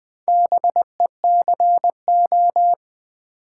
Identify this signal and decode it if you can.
Morse code, heard as a keyed tone, about 20 wpm, 700 Hz. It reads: BECO
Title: Build A Word - 2 Letters At A Time - 6-Letter Words - 20wpm